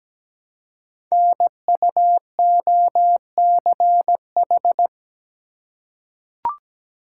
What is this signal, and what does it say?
Morse code, heard as a keyed tone, about 17 wpm, 700 Hz. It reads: NUOCH
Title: Code Groups - 5 Letters - 17wpm